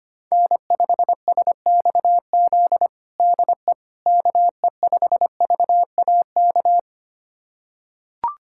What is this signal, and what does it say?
Morse code, heard as a keyed tone, about 25 wpm, 700 Hz. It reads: N5SXZ DE KE5VAK
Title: USA Callsign DE Callsign 25wpm